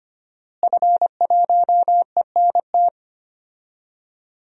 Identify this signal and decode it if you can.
Morse code, heard as a keyed tone, about 25 wpm, 700 Hz. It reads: F1ENT
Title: DXCC-100 Call Signs - 5 Characters - Repeated 2x - 25wpm